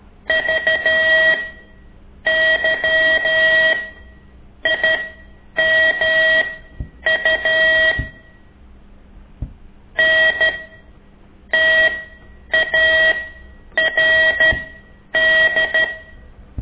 Morse code, heard as a keyed tone, about 13 wpm, spaced as for 8 wpm, 1900 Hz. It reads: VYIMU NTARD
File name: ik1weg2 / CORSI CW